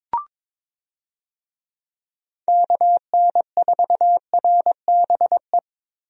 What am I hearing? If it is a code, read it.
Morse code, heard as a keyed tone, about 22 wpm, 700 Hz. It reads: KN4RBE
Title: USA Call Signs - 6 Characters - Repeated 3x - 22wpm